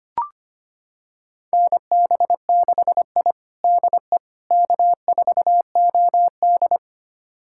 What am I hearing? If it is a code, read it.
Morse code, heard as a keyed tone, about 25 wpm, 700 Hz. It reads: NB6I DE K4OD
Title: USA Callsign DE Callsign - 4 Characters - 25wpm